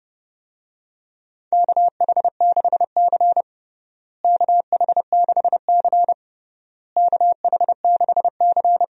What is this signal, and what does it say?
Morse code, heard as a keyed tone, about 30 wpm, 700 Hz. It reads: KH6C KH6C KH6C
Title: DXCC-100 Call Signs - 4 Characters - Repeated 3x - 30wpm